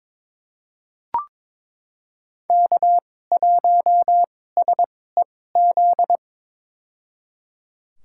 Morse code, heard as a keyed tone, about 22 wpm, spaced as for 16 wpm, 700 Hz. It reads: K1SEZ
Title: USA Call Signs - 2x Character Spacing - 22wpm